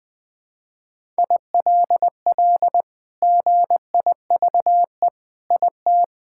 Morse code, heard as a keyed tone, about 20 wpm, 700 Hz. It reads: ILL GIVE IT
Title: Sentences with 2-Word Phrases - 20wpm